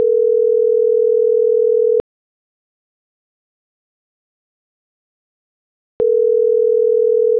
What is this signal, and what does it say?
Ringback tone (North America), 440 + 480 Hz.